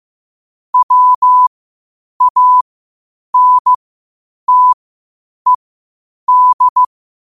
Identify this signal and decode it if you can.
Morse code, heard as a keyed tone, about 15 wpm, spaced as for 8 wpm, 1000 Hz. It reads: WANTED
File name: qotdfiles / 8wpm